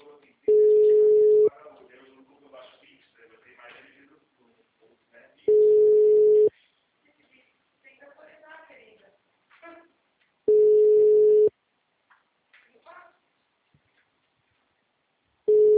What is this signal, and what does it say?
Ringback tone (Europe, 425 Hz), with speech.
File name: out- -2020- - - .514